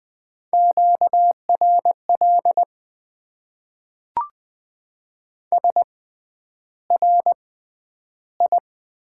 Morse code, heard as a keyed tone, about 20 wpm, spaced as for 10 wpm, 700 Hz. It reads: QRL SRI
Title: Top 90 QSO Elements - 6x Character Spacing - 20wpm